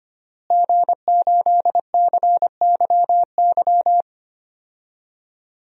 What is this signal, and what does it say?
Morse code, heard as a keyed tone, about 25 wpm, 700 Hz. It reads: G8CYY
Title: DXCC-100 Call Signs - 5 Characters - Repeated 3x - 25wpm